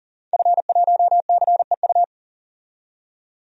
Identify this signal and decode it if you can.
Morse code, heard as a keyed tone, about 40 wpm, 700 Hz. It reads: F1CEU
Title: DXCC-100 Call Signs - 5 Characters - 40wpm